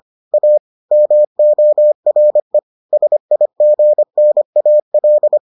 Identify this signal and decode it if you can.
Morse code, heard as a keyed tone, about 25 wpm, 600 Hz. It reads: A MORE SIGNAL